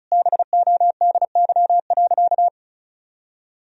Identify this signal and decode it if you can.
Morse code, heard as a keyed tone, about 36 wpm, 700 Hz. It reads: BODY.
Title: Sentences from Top 1000 Words - Review Part 4 - 35wpm